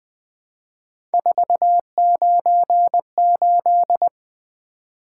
Morse code, heard as a keyed tone, about 20 wpm, 700 Hz. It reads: 498
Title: Three Letter-Number Combinations - 20wpm